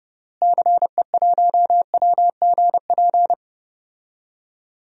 Morse code, heard as a keyed tone, about 29 wpm, 700 Hz. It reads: CE1WGP
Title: DXCC-100 Call Signs - 6 Characters - Repeated 3x - 30wpm